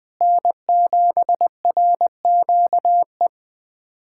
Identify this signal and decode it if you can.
Morse code, heard as a keyed tone, about 21 wpm, 700 Hz. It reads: N7RQE
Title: USA Callsign DE Callsign - 5 Characters - 20wpm